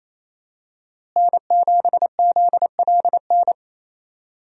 Morse code, heard as a keyed tone, about 27 wpm, 700 Hz. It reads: N7ZLN
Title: USA Call Signs - 5 Characters - Repeated 3x - 28wpm